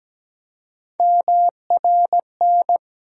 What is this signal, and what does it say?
Morse code, heard as a keyed tone, about 17 wpm, 700 Hz. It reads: MRN
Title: USA Call Sign Postfixes - 3 Letters 17wpm